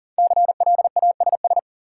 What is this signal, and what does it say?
Morse code, heard as a keyed tone, about 40 wpm, 700 Hz. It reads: CLASS
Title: Sentences from Top 1000 Words - Review Part 2 - New Word Drill - 40wpm